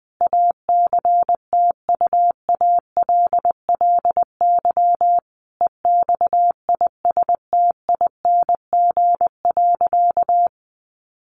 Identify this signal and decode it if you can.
Morse code, heard as a keyed tone, about 20 wpm, 700 Hz. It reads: ACTUALLY EXISTING.
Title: Sentences from Top 1500 Words 20wpm